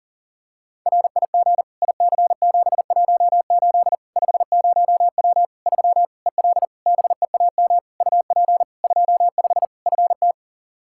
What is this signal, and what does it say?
Morse code, heard as a keyed tone, about 40 wpm, 700 Hz. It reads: RIG IC718 50W 3 EL BEAM UP 25 FT
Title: Practice QSO - Exchange 2 - Info - 40wpm